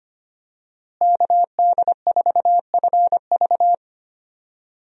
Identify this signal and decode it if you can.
Morse code, heard as a keyed tone, about 25 wpm, 700 Hz. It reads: KD4FV
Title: USA Call Signs - 5 Characters - Repeated 2x - 25wpm